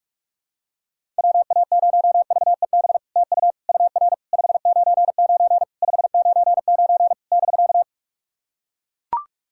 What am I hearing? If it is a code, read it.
Morse code, heard as a keyed tone, about 44 wpm, 700 Hz. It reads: WA0VEB TU UR 599 599 <BK>
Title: SOTA QSO Practice 45wpm